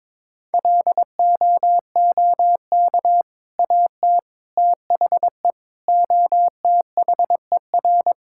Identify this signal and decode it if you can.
Morse code, heard as a keyed tone, about 22 wpm, 700 Hz. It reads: LOOK AT THE OTHER
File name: practice / 22wpm / Sentences with 2-Word Phrases - 22wpm